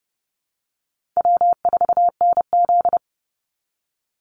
Morse code, heard as a keyed tone, about 30 wpm, 700 Hz. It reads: W4NZ
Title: Easy Call Signs - 30wpm